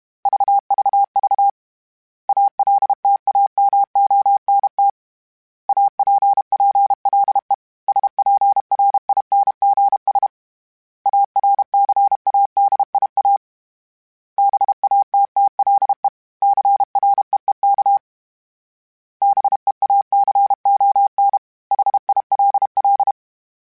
Morse code, heard as a keyed tone, about 32 wpm, 800 Hz. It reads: VVV ALTAMONT APPLE SPRINGS ARCADIA BATTLE CREEK BEACON HILL
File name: fromSMs / FromSMs mp3s / Cities@30